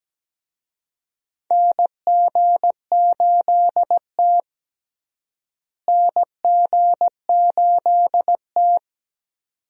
Morse code, heard as a keyed tone, about 17 wpm, 700 Hz. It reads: NG8T NG8T
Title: USA Call Signs - 4 Characters - Repeated 2x - 17wpm